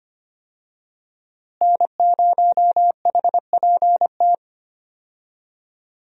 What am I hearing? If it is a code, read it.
Morse code, heard as a keyed tone, about 25 wpm, 700 Hz. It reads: N0HPT
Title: USA Call Signs - 5 Characters - Repeated 3x - 25wpm